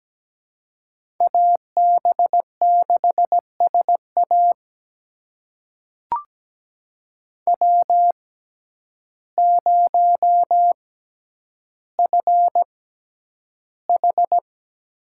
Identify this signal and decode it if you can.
Morse code, heard as a keyed tone, about 17 wpm, spaced as for 9 wpm, 700 Hz. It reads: AB6SA W0FH
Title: USA Call Signs - 5 Characters - 6x Character Spacing - 17wpm